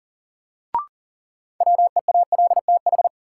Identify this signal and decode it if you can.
Morse code, heard as a keyed tone, about 40 wpm, 700 Hz. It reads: WEALTH
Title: Sentences from Top 1200 Words - New Word Drill Alt - 40wpm